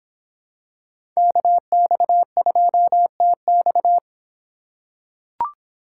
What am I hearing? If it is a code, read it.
Morse code, heard as a keyed tone, about 25 wpm, 700 Hz. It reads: KX2TX